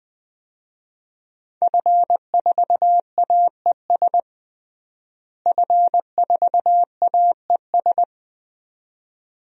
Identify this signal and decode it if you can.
Morse code, heard as a keyed tone, about 20 wpm, 700 Hz. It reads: F4AES F4AES
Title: DXCC-100 Call Signs - 5 Characters - Repeated 2x - 20wpm